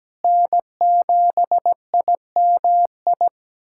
Morse code, heard as a keyed tone, about 17 wpm, 700 Hz. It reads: N7IMI
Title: K1USN SST QSO Practice - 17wpm